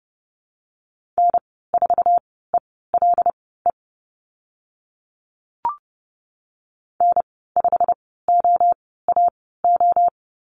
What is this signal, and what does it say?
Morse code, heard as a keyed tone, about 30 wpm, spaced as for 17 wpm, 700 Hz. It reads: N4ELE N5OAO
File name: Lesson 15g - Number 5 - Call Signs 30wpm